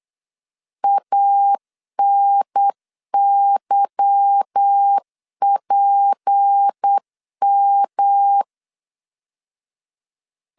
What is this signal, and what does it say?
Morse code, heard as a keyed tone, about 8 wpm, 800 Hz. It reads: ANYPM
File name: LEZIONI / 8 YJ45 / 8 Y 8wpm 3245